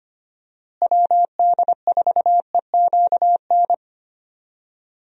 Morse code, heard as a keyed tone, about 25 wpm, 700 Hz. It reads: WD4EQN
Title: USA Call Signs - 6 Characters - Repeated 3x - 25wpm